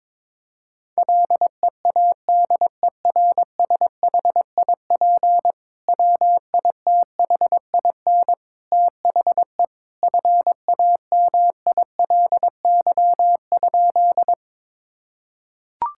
Morse code, heard as a keyed tone, about 22 wpm, 700 Hz. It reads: LEADERSHIP WITHIN THE FAMILY?
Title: Sentences from Top 1600 Words 22wpm